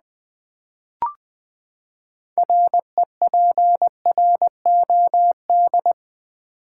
Morse code, heard as a keyed tone, about 20 wpm, 700 Hz. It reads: REPROD
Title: Build A Word - 6 Letters At A Time - 12-Letter Words - 20wpm